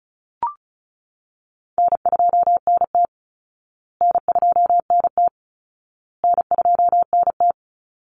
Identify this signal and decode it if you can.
Morse code, heard as a keyed tone, about 34 wpm, 700 Hz. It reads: N2NT N2NT N2NT
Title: Easy Call Signs Repeated 3x - 35wpm